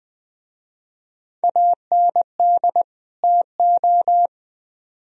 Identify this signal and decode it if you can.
Morse code, heard as a keyed tone, about 20 wpm, 700 Hz. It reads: AND TO
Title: Speed-Racing - Sets of 2 Words 17wpm